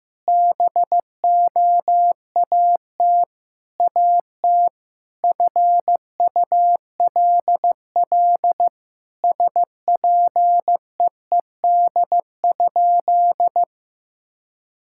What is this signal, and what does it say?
Morse code, heard as a keyed tone, about 15 wpm, 700 Hz. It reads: BOAT AT FULL SPEED?
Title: Sentences from Top 1000 Words - Review Part 5 - 15wpm